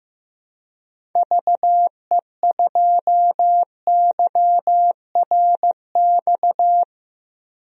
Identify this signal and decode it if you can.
Morse code, heard as a keyed tone, about 15 wpm, 700 Hz. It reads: VE2YRX
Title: DXCC-100 Call Signs - 6 Characters - Repeated 3x - 15wpm